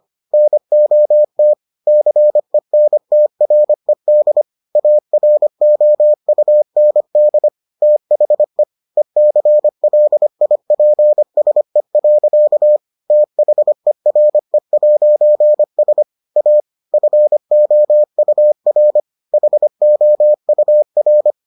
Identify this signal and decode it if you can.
Morse code, heard as a keyed tone, about 25 wpm, 600 Hz. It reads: NOT CENTRED AROUND THE ECLIPSE. THERE'S A FOUR HOUR